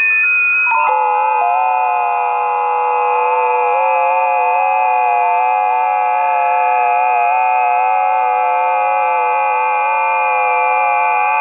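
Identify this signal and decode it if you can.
FT8.
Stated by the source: Busy 20 m band FT8 recording.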